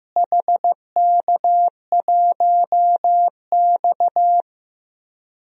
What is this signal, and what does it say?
Morse code, heard as a keyed tone, about 15 wpm, 700 Hz. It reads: HK1X
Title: DXCC-100 Call Signs - 4 Characters - Repeated 3x - 15wpm